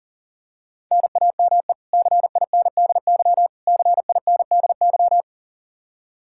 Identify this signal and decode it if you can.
Morse code, heard as a keyed tone, about 40 wpm, 700 Hz. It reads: NAME CINDY CINDY
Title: K1USN SST QSO Practice 40wpm